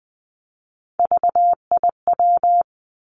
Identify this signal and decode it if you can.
Morse code, heard as a keyed tone, about 19 wpm, 700 Hz. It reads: VIW
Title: Three Challenging Letter-Number Combinations 20wpm